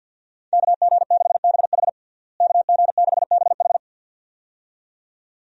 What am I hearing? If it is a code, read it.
Morse code, heard as a keyed tone, about 50 wpm, 700 Hz. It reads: KG6BH KG6BH
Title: USA Call Signs - 5 Characters - Repeated 2x - 50wpm